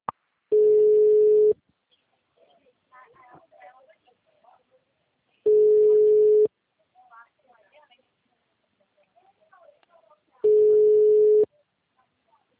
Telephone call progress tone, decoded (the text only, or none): ringback tone